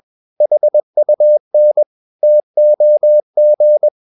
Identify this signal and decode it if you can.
Morse code, heard as a keyed tone, about 21 wpm, 600 Hz. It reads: HUN TOG
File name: Ferie QRS 21wpm